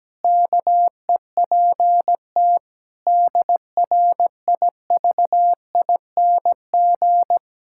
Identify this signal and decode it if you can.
Morse code, heard as a keyed tone, about 17 wpm, 700 Hz. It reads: KEPT DRIVING
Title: Pangram Challenge v2 - 17wpm